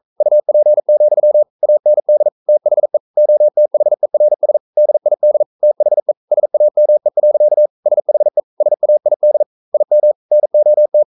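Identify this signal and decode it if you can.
Morse code, heard as a keyed tone, about 42 wpm, 600 Hz. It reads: UP, AND THE OTHERS DID THE SAME. SHE SAID IM NOT
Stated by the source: Morse code practice at 41 wpm, from CW med Gnister.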